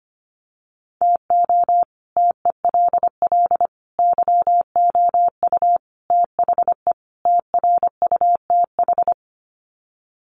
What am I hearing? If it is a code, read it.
Morse code, heard as a keyed tone, about 25 wpm, 700 Hz. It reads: TO TELL YOU THE TRUTH
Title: Sets of 5 Words 25wpm